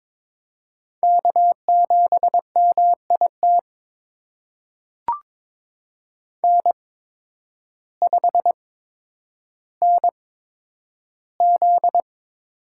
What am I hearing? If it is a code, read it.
Morse code, heard as a keyed tone, about 22 wpm, spaced as for 10 wpm, 700 Hz. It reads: K7MIT N5NZ
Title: USA Call Signs - 5 Characters - 8x Character Spacing - 22wpm